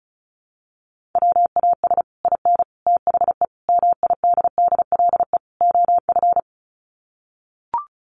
Morse code, heard as a keyed tone, about 34 wpm, 700 Hz. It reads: WAS IN THE MIDDLE OF
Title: Sets of 5 Words Encore 35wpm